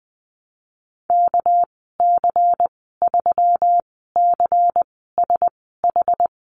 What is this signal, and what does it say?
Morse code, heard as a keyed tone, about 20 wpm, spaced as for 14 wpm, 700 Hz. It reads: KC3CSH
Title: 200 Lesson 32g - Letter k - Call Signs 20wpm